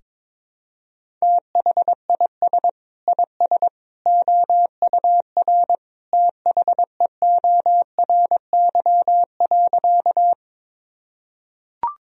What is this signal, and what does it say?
Morse code, heard as a keyed tone, about 22 wpm, 700 Hz. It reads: THIS IS OUR THEORY.
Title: Sentences from Top 1000 Words - Review Part 3 - 22wpm